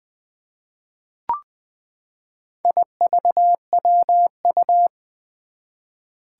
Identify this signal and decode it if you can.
Morse code, heard as a keyed tone, about 20 wpm, 700 Hz. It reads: IVWU
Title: Four Challenging Letter-Number Combinations 20wpm